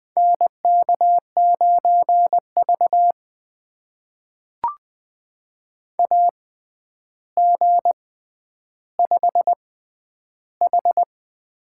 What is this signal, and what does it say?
Morse code, heard as a keyed tone, about 20 wpm, spaced as for 10 wpm, 700 Hz. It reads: NK9V AG5H
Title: USA Call Signs - 4 Characters - 6x Character Spacing - 20wpm